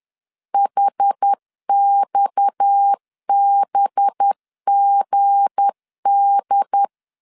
Morse code, heard as a keyed tone, about 10 wpm, 800 Hz. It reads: HXBGD